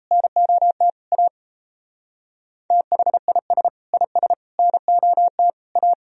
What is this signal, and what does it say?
Morse code, heard as a keyed tone, about 34 wpm, 690 Hz. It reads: NOT A THIS IS NOT A
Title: Speed-Racing - Sets of 4 Words 28wpm